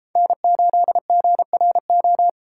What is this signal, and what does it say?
Morse code, heard as a keyed tone, about 33 wpm, 700 Hz. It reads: N8GRO